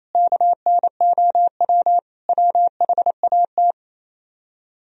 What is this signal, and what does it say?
Morse code, heard as a keyed tone, about 36 wpm, 700 Hz. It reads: KNOW WHAT